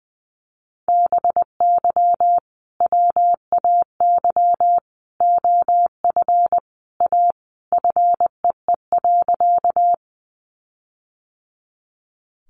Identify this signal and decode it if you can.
Morse code, heard as a keyed tone, about 20 wpm, 700 Hz. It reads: BY WAY OF A FEE.
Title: Sentences from Top 1500 Words 20wpm